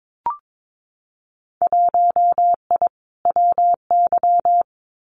Morse code, heard as a keyed tone, about 22 wpm, 700 Hz. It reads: 1I WY
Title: Winter Field Day Exchanges Encore 22wpm